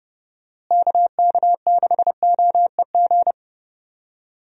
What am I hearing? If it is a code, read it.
Morse code, heard as a keyed tone, about 31 wpm, 700 Hz. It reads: KK6OEG
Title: USA Call Signs - 6 Characters - Repeated 3x - 30wpm